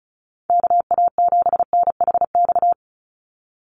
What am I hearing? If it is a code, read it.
Morse code, heard as a keyed tone, about 36 wpm, 700 Hz. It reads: KA7NHX
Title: USA Call Signs 35wpm